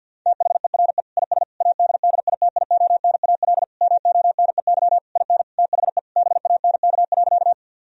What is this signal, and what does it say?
Morse code, heard as a keyed tone, about 50 wpm, 700 Hz. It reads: THERE IS ADDITIONAL MONEY IN THE BANK.
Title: Sentences from Top 1000 Words - Review Part 4 - 50wpm